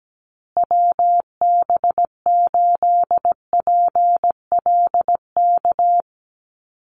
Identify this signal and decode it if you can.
Morse code, heard as a keyed tone, about 17 wpm, 700 Hz. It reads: WB8PLK
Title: USA Call Signs Repeated 2x 17wpm